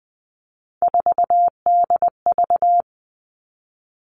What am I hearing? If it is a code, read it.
Morse code, heard as a keyed tone, about 20 wpm, 700 Hz. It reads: 4DV
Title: Three Challenging Letter-Number Combinations 20wpm